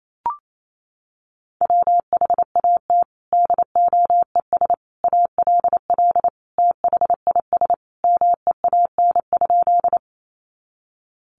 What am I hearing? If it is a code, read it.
Morse code, heard as a keyed tone, about 28 wpm, 700 Hz. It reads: WHAT DOES ALL THIS MEAN?
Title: Sentences from Top 200 Words 28wpm